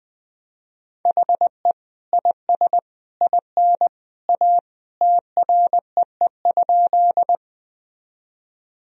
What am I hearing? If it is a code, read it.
Morse code, heard as a keyed tone, about 20 wpm, 700 Hz. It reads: HE IS IN A TREE?
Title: Sentences from Top 1000 Words - Review Part 1 - 20wpm